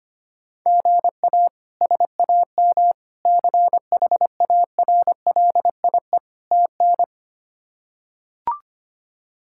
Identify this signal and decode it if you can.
Morse code, heard as a keyed tone, about 25 wpm, 700 Hz. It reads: GA SAM CHARLIE TN
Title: K1USN SST QSO Practice - 25wpm